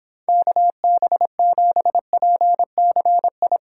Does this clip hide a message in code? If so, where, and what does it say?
Yes: Morse code from 0.3 to 3.6 s, heard as a keyed tone, about 31 wpm, 700 Hz. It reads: KB7PCI